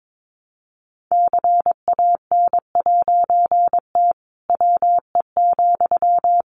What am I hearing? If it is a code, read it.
Morse code, heard as a keyed tone, about 22 wpm, 700 Hz. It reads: CAN'T WE,
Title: A Sound Of Thunder 22wpm